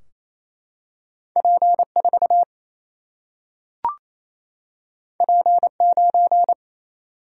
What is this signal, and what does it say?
Morse code, heard as a keyed tone, about 28 wpm, 700 Hz. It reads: P4 P9
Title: Single-Letter Single-Number 28wpm